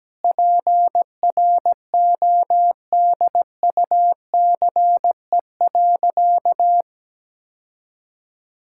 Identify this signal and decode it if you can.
Morse code, heard as a keyed tone, about 17 wpm, 700 Hz. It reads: PRODUCE.
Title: Sentences from Top 1000 Words - Review Part 5 - 17wpm